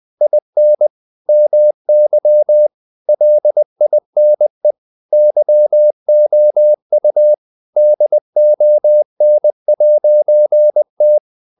Morse code, heard as a keyed tone, about 20 wpm, 600 Hz. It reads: IN MY LINE YOU DON'T
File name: CW / Practice / EnglishQuotations0320